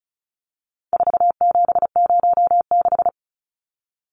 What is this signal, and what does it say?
Morse code, heard as a keyed tone, about 34 wpm, 700 Hz. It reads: 4706
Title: Numbers - 4-Digit 35wpm